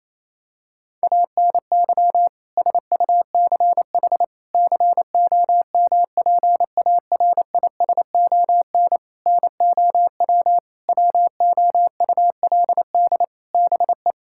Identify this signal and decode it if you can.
Morse code, heard as a keyed tone, about 28 wpm, 700 Hz. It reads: ANY SUCH COMPARISON NOW WOULD BE
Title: Sentences from Top 1600 Words 28wpm